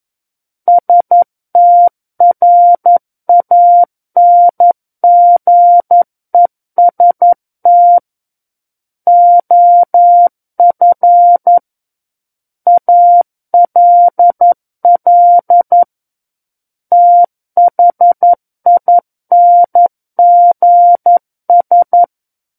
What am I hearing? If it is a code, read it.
Morse code, heard as a keyed tone, about 11 wpm, 700 Hz. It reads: STRANGEST OF ALL THINGS
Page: War of the Worlds - 02-Chapter 2 - 11 WPM